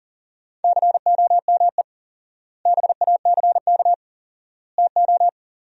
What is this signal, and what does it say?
Morse code, heard as a keyed tone, about 37 wpm, 700 Hz. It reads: COME BACK TO
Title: 4x Word-Spacing - Sentences from Top 500 Words - 40wpm